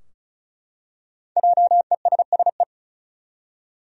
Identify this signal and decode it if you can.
Morse code, heard as a keyed tone, about 34 wpm, 700 Hz. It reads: JESSE
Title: USA Common Names - 35wpm